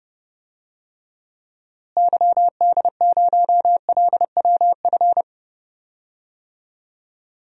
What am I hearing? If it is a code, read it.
Morse code, heard as a keyed tone, about 30 wpm, 700 Hz. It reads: YD0LWF
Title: DXCC-100 Call Signs - 6 Characters - Repeated 3x - 30wpm